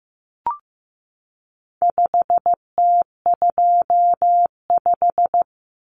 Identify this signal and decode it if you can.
Morse code, heard as a keyed tone, about 15 wpm, 700 Hz. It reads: 5T25
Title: Common Cut Numbers - 4-Digit 15wpm